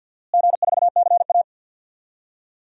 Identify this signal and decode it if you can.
Morse code, heard as a keyed tone, about 46 wpm, 700 Hz. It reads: G4CA
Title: DXCC-100 Call Signs - 4 Characters - Repeated 2x - 50wpm